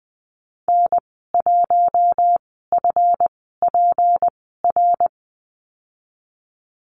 Morse code, heard as a keyed tone, about 20 wpm, spaced as for 15 wpm, 700 Hz. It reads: N1FPR
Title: 167 Lesson 27g - Number 7 - Call Signs 20wpm